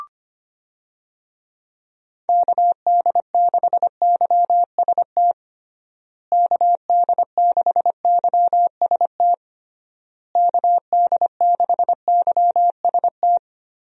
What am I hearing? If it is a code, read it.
Morse code, heard as a keyed tone, about 25 wpm, 700 Hz. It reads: KD6YST KD6YST KD6YST
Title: USA Call Signs - 6 Characters - Repeated 3x - 25wpm